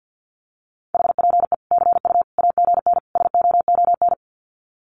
Morse code, heard as a keyed tone, about 50 wpm, 700 Hz. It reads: HPE CU AGN SOON
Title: Common QSO Phrases 50wpm